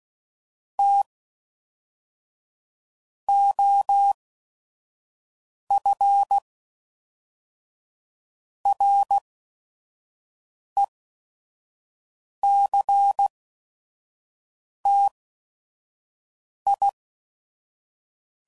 Morse code, heard as a keyed tone, about 16 wpm, spaced as for 5 wpm, 800 Hz. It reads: T OF RECTI
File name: CW Code Groups @5WPM